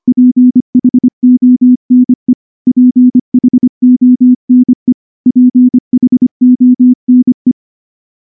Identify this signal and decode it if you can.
Morse code, heard as a keyed tone, about 23 wpm, 250 Hz. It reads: PHONE PHONE PHONE
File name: morse / morse-training-25WPM